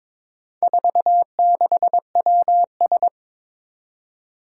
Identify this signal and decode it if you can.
Morse code, heard as a keyed tone, about 21 wpm, 700 Hz. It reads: 46WS
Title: Four Challenging Letter-Number Combinations 22wpm